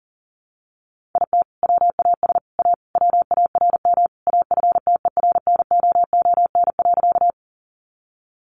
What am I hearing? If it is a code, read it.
Morse code, heard as a keyed tone, about 40 wpm, 700 Hz. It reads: IT WAS A WARM AFTERNOON.
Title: Sentences from Top 1500 Words 40wpm